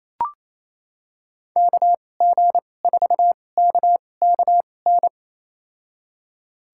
Morse code, heard as a keyed tone, about 28 wpm, spaced as for 20 wpm, 700 Hz. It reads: KG4KKN
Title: USA Call Signs - 6 Characters - 2x Character Spacing - 28wpm